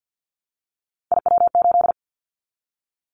Morse code, heard as a keyed tone, about 50 wpm, 700 Hz. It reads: IW8
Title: Callsign Prefixes for Top 25 DXCC - 50wpm